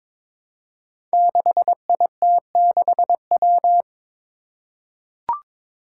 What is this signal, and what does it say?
Morse code, heard as a keyed tone, about 21 wpm, 700 Hz. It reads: 6IT6W
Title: Five Challenging Letter-Number Combinations 22wpm